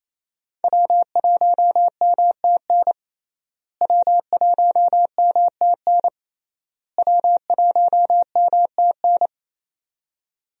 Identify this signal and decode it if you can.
Morse code, heard as a keyed tone, about 27 wpm, 700 Hz. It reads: W1MTN W1MTN W1MTN
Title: USA Call Signs - 5 Characters - Repeated 3x - 28wpm